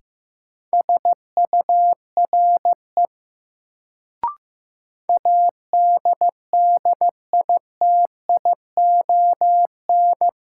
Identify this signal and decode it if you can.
Morse code, heard as a keyed tone, about 15 wpm, 700 Hz. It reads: SURE ADDITION
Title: Sets of 3 Words - New Word Drill - 15wpm